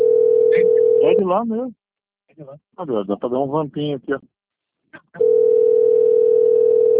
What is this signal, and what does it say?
Ringback tone (North America), 440 + 480 Hz, with speech.
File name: rg-1600-unknown- - - .26179